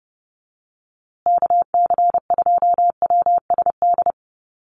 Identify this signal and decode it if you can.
Morse code, heard as a keyed tone, about 30 wpm, 700 Hz. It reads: KC2WSD
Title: USA Call Signs Repeated 3x 30wpm